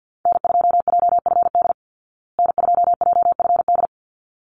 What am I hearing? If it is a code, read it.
Morse code, heard as a keyed tone, about 48 wpm, 700 Hz. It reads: N2JFD N2JFD
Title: Call Signs Repeated 3x - 50wpm